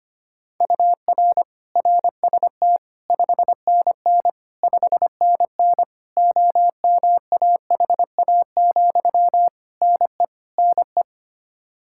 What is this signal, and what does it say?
Morse code, heard as a keyed tone, about 25 wpm, 700 Hz. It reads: UR RST 5NN 5NN OMAHA, NE NE
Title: Practice QSO - Exchange 1 - Beginning - 25wpm